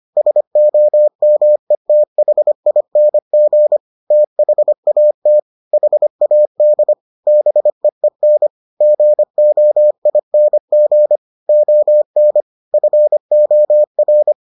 Morse code, heard as a keyed tone, about 25 wpm, 600 Hz. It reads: SOMETHING THAT HAD BEEN GOING ON FOR